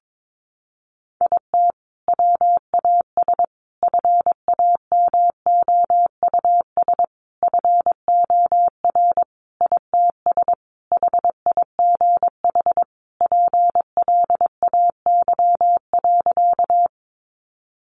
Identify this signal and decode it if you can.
Morse code, heard as a keyed tone, about 22 wpm, 700 Hz. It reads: IT WAS FAMOUS FOR ITS HIGH PLAY.
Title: Sentences from Top 1500 Words 22wpm